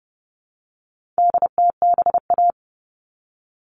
Morse code, heard as a keyed tone, about 30 wpm, 700 Hz. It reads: DTBA
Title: All Cut Numbers - 4-Digit 30wpm